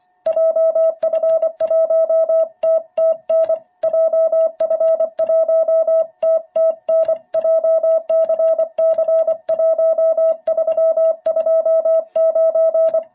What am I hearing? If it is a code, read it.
Morse code, heard as a keyed tone, about 25 wpm, 660 Hz. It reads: JF1TTN JF1TTN JCC1329
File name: jh8cbh / 2021cw